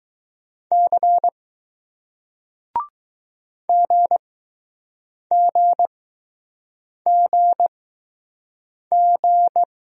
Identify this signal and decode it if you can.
Morse code, heard as a keyed tone, about 20 wpm, spaced as for 6 wpm, 700 Hz. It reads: C GGGG